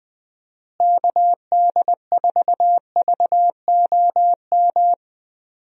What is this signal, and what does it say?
Morse code, heard as a keyed tone, about 21 wpm, 700 Hz. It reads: KD4VOM
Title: USA Call Signs - 6 Characters - Repeated 3x - 20wpm